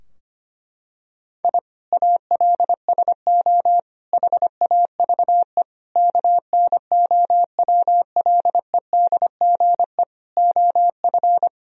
Morse code, heard as a keyed tone, about 25 wpm, 700 Hz. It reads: I ALSO HAVE KNOWLEDGE OF
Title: Sentences from Top 1000 Words - Review Part 3 - 25wpm